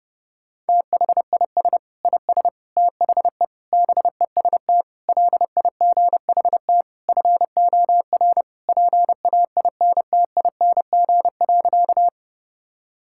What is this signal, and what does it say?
Morse code, heard as a keyed tone, about 30 wpm, 700 Hz. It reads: THIS IS THE BEST LIGHT FOR PAINTING.
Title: Sentences from Top 1000 Words - Review Part 6 - 30wpm